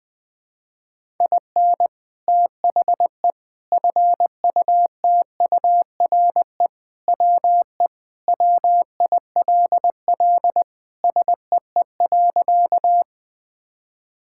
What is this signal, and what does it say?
Morse code, heard as a keyed tone, about 20 wpm, 700 Hz. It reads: IN THE FUTURE WE WILL SEE.
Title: Sentences with 3-Word Phrases - 20wpm